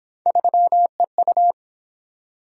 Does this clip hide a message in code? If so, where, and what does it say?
Yes: Morse code from 0.3 to 1.5 s, heard as a keyed tone, about 29 wpm, 700 Hz. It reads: 3EU